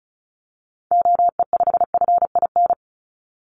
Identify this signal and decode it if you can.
Morse code, heard as a keyed tone, about 36 wpm, 700 Hz. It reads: OE5FIN
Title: Call Signs - 35wpm